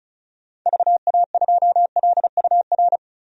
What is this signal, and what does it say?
Morse code, heard as a keyed tone, about 34 wpm, 700 Hz. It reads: VA2LUR
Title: DXCC-100 Call Signs - 6 Characters - 35wpm